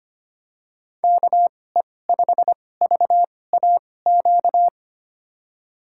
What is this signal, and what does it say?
Morse code, heard as a keyed tone, about 25 wpm, spaced as for 18 wpm, 690 Hz. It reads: KE5VAQ
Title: USA Call Signs - 6 Characters - 2x Character Spacing - 25wpm